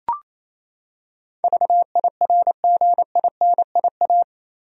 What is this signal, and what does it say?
Morse code, heard as a keyed tone, about 28 wpm, 700 Hz. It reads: VIRGINIA
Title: USA Common Names - 28wpm